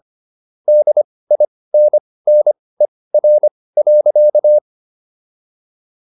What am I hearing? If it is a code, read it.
Morse code, heard as a keyed tone, about 25 wpm, spaced as for 18 wpm, 600 Hz. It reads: DINNER.